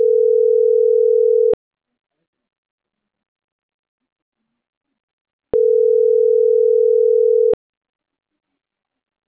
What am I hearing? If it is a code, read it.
Ringback tone (North America), 440 + 480 Hz.